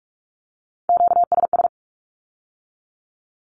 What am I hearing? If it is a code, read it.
Morse code, heard as a keyed tone, about 46 wpm, 700 Hz. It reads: QSS
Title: Q-Codes with Spoken Meaning - 45wpm